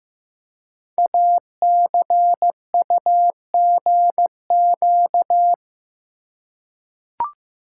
Morse code, heard as a keyed tone, about 15 wpm, 700 Hz. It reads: ACUGQ
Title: Code Groups - 5 Letters - 15wpm